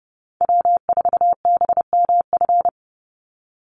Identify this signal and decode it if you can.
Morse code, heard as a keyed tone, about 30 wpm, 700 Hz. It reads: W4BMF
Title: USA Call Signs 30wpm